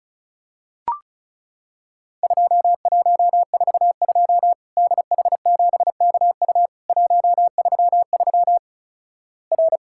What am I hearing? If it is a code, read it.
Morse code, heard as a keyed tone, about 36 wpm, 700 Hz. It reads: 2142 DH7KU 133 R
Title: Worked All Europe DX Contest - QTC Exchange - 35wpm